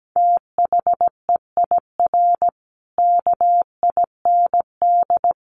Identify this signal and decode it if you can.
Morse code, heard as a keyed tone, about 17 wpm, 700 Hz. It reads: THEIR KIND
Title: Sentences from Top 1500 Words 17wpm